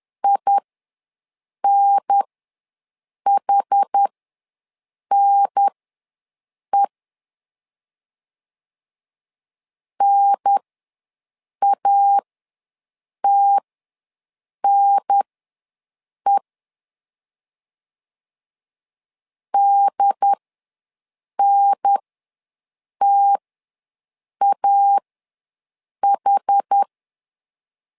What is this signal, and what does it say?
Morse code, heard as a keyed tone, about 10 wpm, spaced as for 6 wpm, 800 Hz. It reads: INHNE NATNE DNTAH
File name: 2 UHDN ws 10wpm